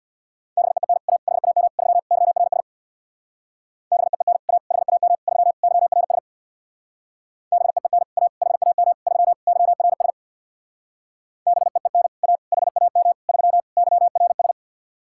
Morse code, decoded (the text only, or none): BEEN A HAM 3 YRS BEEN A HAM 3 YRS BEEN A HAM 3 YRS BEEN A HAM 3 YRS